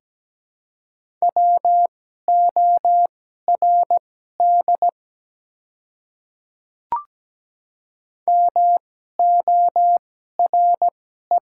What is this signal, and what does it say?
Morse code, heard as a keyed tone, about 17 wpm, spaced as for 12 wpm, 700 Hz. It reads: WORD MORE
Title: 2x Character Spacing - Top 100 Words - 17wpm